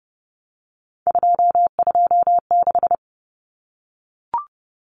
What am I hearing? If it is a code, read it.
Morse code, heard as a keyed tone, about 30 wpm, 700 Hz. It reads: 226
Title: Common Cut Numbers - 3-Digit 30wpm